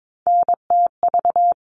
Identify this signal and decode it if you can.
Morse code, heard as a keyed tone, about 22 wpm, 700 Hz. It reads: NTV
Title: All Cut Numbers - 3-Digit 22wpm